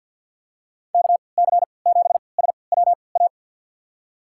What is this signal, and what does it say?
Morse code, heard as a keyed tone, about 52 wpm, spaced as for 28 wpm, 700 Hz. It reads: KC7SWA
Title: USA Call Signs - 6 Characters - 3x Character Spacing - 50wpm